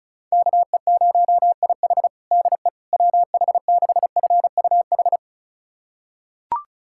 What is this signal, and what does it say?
Morse code, heard as a keyed tone, about 36 wpm, 700 Hz. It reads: KE0IH DE WH6FUH
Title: USA Callsign DE Callsign 35wpm